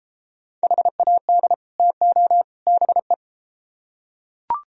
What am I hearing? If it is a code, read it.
Morse code, heard as a keyed tone, about 30 wpm, 700 Hz. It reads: HAD TO BE